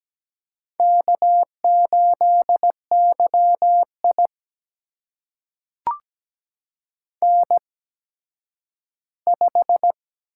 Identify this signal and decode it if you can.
Morse code, heard as a keyed tone, about 17 wpm, spaced as for 10 wpm, 700 Hz. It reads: K8YI N5
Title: USA Call Signs - 8x Character Spacing - 17wpm